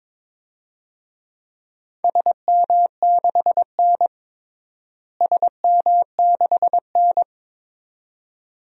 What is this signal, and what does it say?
Morse code, heard as a keyed tone, about 22 wpm, 700 Hz. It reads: SM6N SM6N
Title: DXCC-100 Call Signs - 4 Characters - Repeated 2x - 22wpm